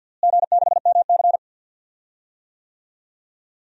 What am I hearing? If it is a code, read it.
Morse code, heard as a keyed tone, about 51 wpm, 700 Hz. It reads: G6MC
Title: DXCC-100 Call Signs - 4 Characters - Repeated 2x - 50wpm